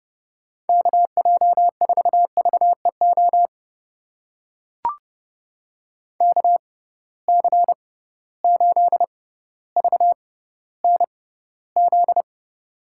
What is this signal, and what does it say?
Morse code, heard as a keyed tone, about 31 wpm, spaced as for 15 wpm, 700 Hz. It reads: KJ4VEO KC8VNZ
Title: USA Call Signs - 6 Characters - 6x Character Spacing - 30wpm